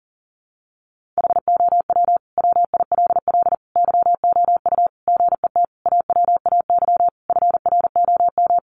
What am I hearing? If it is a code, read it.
Morse code, heard as a keyed tone, about 40 wpm, 700 Hz. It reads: HOW WILL YOU GET AWAY FROM
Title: Sentences from Top 1500 Words 40wpm